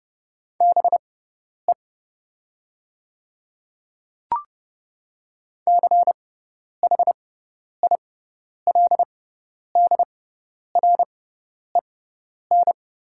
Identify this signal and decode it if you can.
Morse code, heard as a keyed tone, about 31 wpm, spaced as for 10 wpm, 700 Hz. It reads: BE CHILDREN
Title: 6x Character Spacing - Top 500 Words - 30wpm